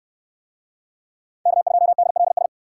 Morse code, heard as a keyed tone, about 68 wpm, 690 Hz. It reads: K2DLS